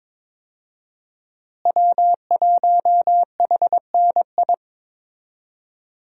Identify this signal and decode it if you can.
Morse code, heard as a keyed tone, about 22 wpm, 700 Hz. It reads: W1HNI
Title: USA Call Signs - 5 Characters - Repeated 2x - 22wpm